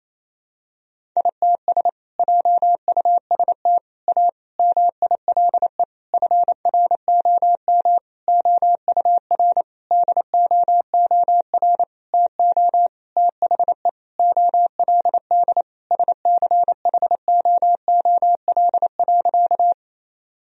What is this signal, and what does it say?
Morse code, heard as a keyed tone, about 28 wpm, 700 Hz. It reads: ITS JUST A MILE FROM OUR DOOR TO THE OLD SCHOOL.
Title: Sentences from Top 1000 Words - Review Part 1 - 28wpm